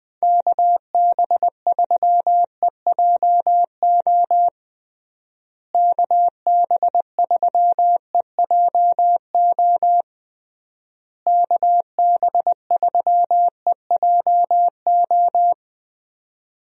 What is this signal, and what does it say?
Morse code, heard as a keyed tone, about 20 wpm, 700 Hz. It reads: KB3EJO KB3EJO KB3EJO
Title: USA Call Signs - 6 Characters - Repeated 3x - 20wpm